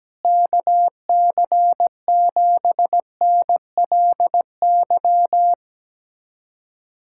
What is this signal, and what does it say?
Morse code, heard as a keyed tone, about 17 wpm, 700 Hz. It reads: KC7NLY
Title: USA Callsign DE Callsign 17wpm